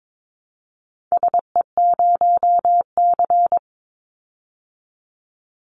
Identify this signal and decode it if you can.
Morse code, heard as a keyed tone, about 22 wpm, 700 Hz. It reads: SE0C
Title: Call Signs Repeated 3x - 22wpm